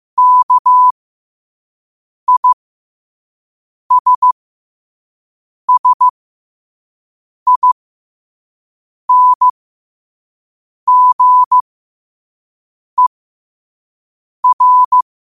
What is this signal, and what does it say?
Morse code, heard as a keyed tone, about 15 wpm, spaced as for 5 wpm, 1000 Hz. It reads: KISSINGER
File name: qotdfiles / 5wpm